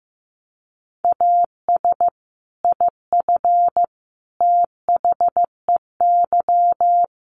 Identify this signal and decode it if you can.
Morse code, heard as a keyed tone, about 15 wpm, 700 Hz. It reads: AS IF THEY
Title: Sets of 4 Words 15wpm